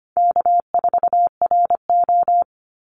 Morse code, heard as a keyed tone, about 25 wpm, 700 Hz. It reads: K4RO
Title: Easy Call Signs - 25wpm